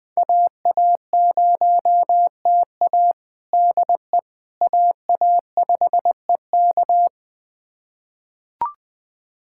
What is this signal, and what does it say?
Morse code, heard as a keyed tone, about 21 wpm, 700 Hz. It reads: AA0TA DE AA5EK
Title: USA Callsign DE Callsign - 5 Characters - 20wpm